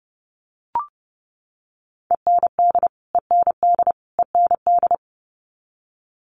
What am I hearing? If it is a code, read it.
Morse code, heard as a keyed tone, about 30 wpm, 700 Hz. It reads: END END END
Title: Top 500 Words Repeated 30wpm